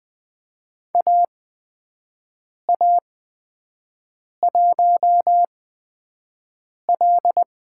Morse code, heard as a keyed tone, about 20 wpm, spaced as for 5 wpm, 700 Hz. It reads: AA1L
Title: USA Call Signs - 5 Characters - 8x Character Spacing - 20wpm